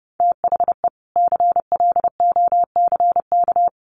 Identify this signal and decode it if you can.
Morse code, heard as a keyed tone, about 30 wpm, 700 Hz. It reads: THE CLOCK
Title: Easy Phrases 30wpm